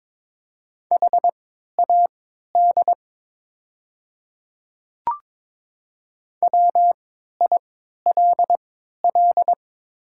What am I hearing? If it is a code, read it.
Morse code, heard as a keyed tone, about 22 wpm, spaced as for 12 wpm, 700 Hz. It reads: HAD WILL
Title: 3x Character Spacing - Top 100 Words - 22wpm